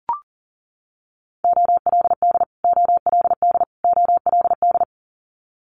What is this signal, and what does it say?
Morse code, heard as a keyed tone, about 38 wpm, 700 Hz. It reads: OLD OLD OLD
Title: Multi-letter Phonemes Repeated 3x 40wpm